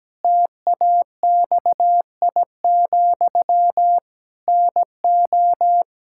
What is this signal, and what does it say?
Morse code, heard as a keyed tone, about 17 wpm, 700 Hz. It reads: TAXI, NO
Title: Pangram Challenge v2 - 17wpm